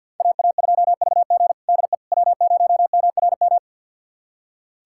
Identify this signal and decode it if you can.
Morse code, heard as a keyed tone, about 46 wpm, 700 Hz. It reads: AA2VG DE W0MRM
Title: USA Callsign DE Callsign - 5 Characters - 50wpm